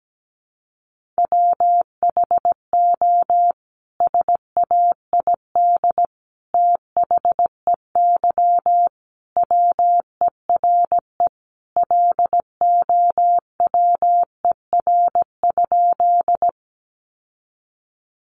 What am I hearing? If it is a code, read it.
Morse code, heard as a keyed tone, about 17 wpm, 700 Hz. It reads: WHO SAID THEY WERE LOWER?
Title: Sentences from Top 700 Words 17wpm